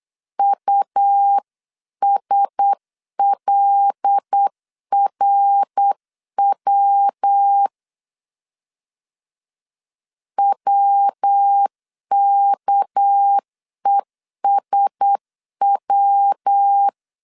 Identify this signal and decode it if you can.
Morse code, heard as a keyed tone, about 8 wpm, 800 Hz. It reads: USLRW WKESW
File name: CW / DET / 1 6 1500C S C 8wpm 5219